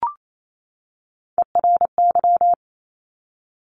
Morse code, heard as a keyed tone, about 28 wpm, 700 Hz. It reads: ERY
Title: Top Three Letter Combinations in Words 28wpm